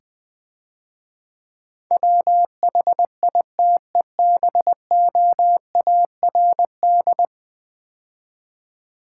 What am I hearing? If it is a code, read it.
Morse code, heard as a keyed tone, about 22 wpm, 700 Hz. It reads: WHITEBOARD
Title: Difficult Words - Speed-Racing with Hints - 20wpm